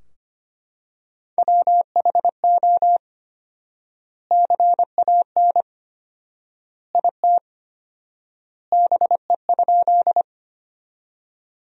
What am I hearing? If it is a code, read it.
Morse code, heard as a keyed tone, about 25 wpm, 700 Hz. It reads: WHO CAN IT BE?
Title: 4x Word-Spacing - Sentences from Top 100 Words - 25wpm